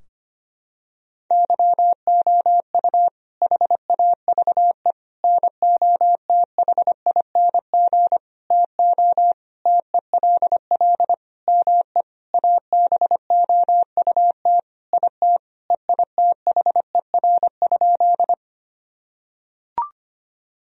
Morse code, heard as a keyed tone, about 25 wpm, 700 Hz. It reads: YOU HAVE NOTHING TO TELL ME ABOUT IT EITHER?
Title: Sentences from Top 1000 Words - Review Part 4 - 25wpm